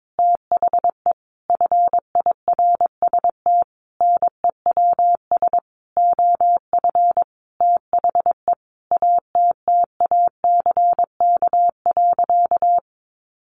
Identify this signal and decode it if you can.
Morse code, heard as a keyed tone, about 22 wpm, 700 Hz. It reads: THE FIRST NEWS OF THE ATTACK.
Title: Sentences from Top 1500 Words 22wpm